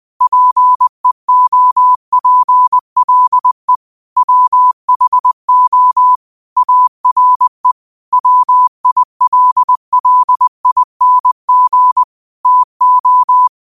Morse code, heard as a keyed tone, about 20 wpm, 1000 Hz. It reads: PEOPLE WHO ARE WILLING TO
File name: qotdfiles / 20wpm